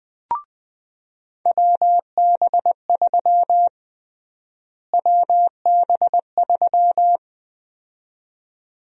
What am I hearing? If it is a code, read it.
Morse code, heard as a keyed tone, about 21 wpm, 700 Hz. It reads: WB3 WB3
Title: USA Call Sign Prefixes - Repeated 2x - 20wpm